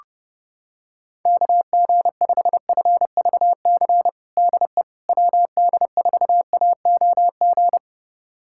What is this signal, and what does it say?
Morse code, heard as a keyed tone, about 30 wpm, 700 Hz. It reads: KG5FVC DE WD4AOG
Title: USA Callsign DE Callsign 30wpm